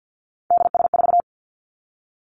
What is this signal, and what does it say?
Morse code, heard as a keyed tone, about 50 wpm, 700 Hz. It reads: DS4
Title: Three Challenging Letter-Number Combinations 50wpm